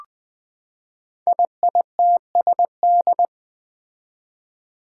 Morse code, heard as a keyed tone, about 20 wpm, 700 Hz. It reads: IITSD
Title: Code Groups - 5 Letters - 20wpm